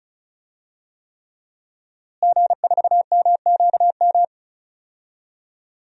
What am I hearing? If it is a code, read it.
Morse code, heard as a keyed tone, about 36 wpm, 700 Hz. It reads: G4MQM
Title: DXCC-100 Call Signs - 5 Characters - Repeated 3x - 35wpm